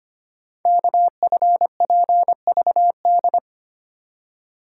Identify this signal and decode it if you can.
Morse code, heard as a keyed tone, about 25 wpm, 700 Hz. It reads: KFPVD
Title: Code Groups - 5 Letters - 25wpm